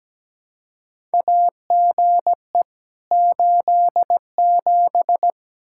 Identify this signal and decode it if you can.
Morse code, heard as a keyed tone, about 17 wpm, 700 Hz. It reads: AGE 87
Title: K1USN SST QSO Practice 17wpm